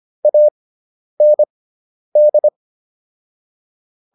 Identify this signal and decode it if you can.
Morse code, heard as a keyed tone, about 25 wpm, spaced as for 10 wpm, 600 Hz. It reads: AND